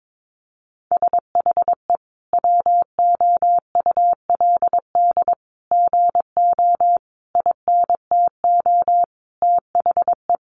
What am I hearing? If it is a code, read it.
Morse code, heard as a keyed tone, about 22 wpm, 700 Hz. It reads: SHE WOULD GO INTO THE
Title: Sentences from Top 1300 Words 22wpm